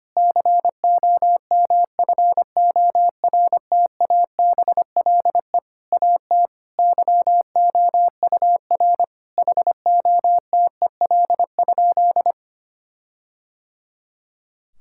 Morse code, heard as a keyed tone, about 25 wpm, 700 Hz. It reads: COMFORTABLE AT YOUR HOTEL?
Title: Sentences from Top 1600 Words 25wpm